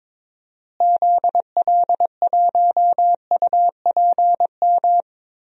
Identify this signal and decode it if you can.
Morse code, heard as a keyed tone, about 22 wpm, 700 Hz. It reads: ZL1UPM
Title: DXCC-100 Call Signs - 6 Characters - Repeated 3x - 22wpm